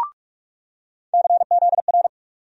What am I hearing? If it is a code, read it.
Morse code, heard as a keyed tone, about 46 wpm, 700 Hz. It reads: CZR
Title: USA Call Sign Postfixes - 3 Letters 45wpm